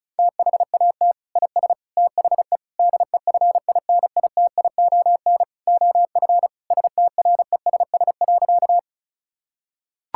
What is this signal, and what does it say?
Morse code, heard as a keyed tone, about 36 wpm, 700 Hz. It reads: THAT IS THE DEFINITION OF STRESS.
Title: Sentences from Top 1000 Words - Review Part 6 - 35wpm